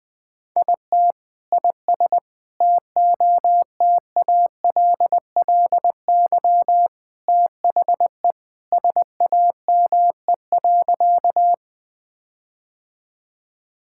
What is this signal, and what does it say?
Morse code, heard as a keyed tone, about 20 wpm, 700 Hz. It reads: IT IS TOTALLY THE SAME.
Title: Sentences from Top 1600 Words 20wpm